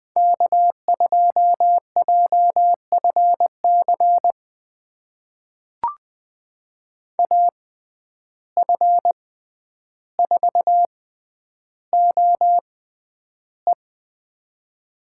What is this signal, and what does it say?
Morse code, heard as a keyed tone, about 21 wpm, spaced as for 10 wpm, 700 Hz. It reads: K2JFC AF4OE
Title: USA Call Signs - 5 Characters - 6x Character Spacing - 20wpm